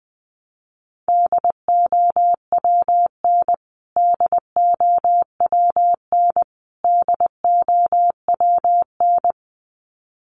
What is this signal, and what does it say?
Morse code, heard as a keyed tone, about 20 wpm, 700 Hz. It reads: DOWN DOWN DOWN
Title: Top 500 Words Repeated 20wpm